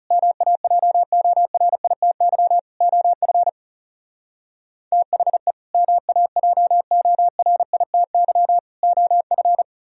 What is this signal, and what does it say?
Morse code, heard as a keyed tone, about 37 wpm, 700 Hz. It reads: MAJORITY OF THE MAJORITY OF
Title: Speed-Racing - Sets of 3 Words 30wpm